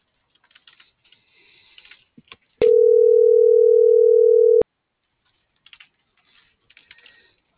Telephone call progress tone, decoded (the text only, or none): ringback tone